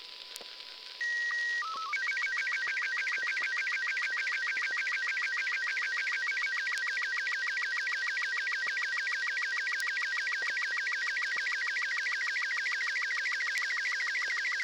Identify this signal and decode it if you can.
SSTV (slow-scan television), Robot 36.
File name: arissat sstv blue